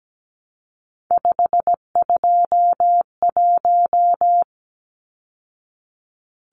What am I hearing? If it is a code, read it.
Morse code, heard as a keyed tone, about 17 wpm, 700 Hz. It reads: 521
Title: Numbers - 3-Digit 17wpm